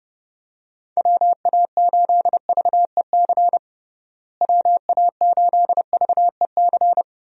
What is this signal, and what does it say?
Morse code, heard as a keyed tone, about 29 wpm, 700 Hz. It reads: WA8VEC WA8VEC
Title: USA Call Signs - 6 Characters - Repeated 2x - 30wpm